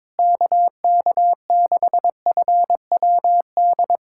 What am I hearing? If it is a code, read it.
Morse code, heard as a keyed tone, about 22 wpm, 700 Hz. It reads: KK6FWD
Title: USA Callsign DE Callsign 22wpm